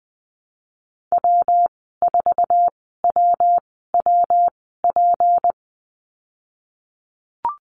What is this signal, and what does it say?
Morse code, heard as a keyed tone, about 19 wpm, spaced as for 14 wpm, 700 Hz. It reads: W4WWP
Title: 173 Lesson 28g - Number 9 - Call Signs 20wpm